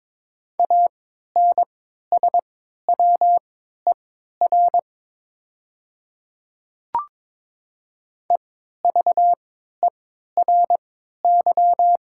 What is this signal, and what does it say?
Morse code, heard as a keyed tone, about 22 wpm, spaced as for 12 wpm, 700 Hz. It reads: ANSWER EVERY
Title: 3x Character Spacing - Top 300 Words - 22wpm